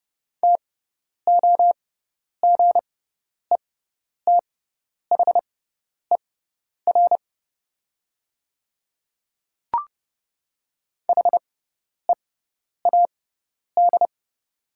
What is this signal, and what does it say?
Morse code, heard as a keyed tone, about 30 wpm, spaced as for 10 wpm, 700 Hz. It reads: TOGETHER HEAD
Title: 6x Character Spacing - Top 300 Words - 30wpm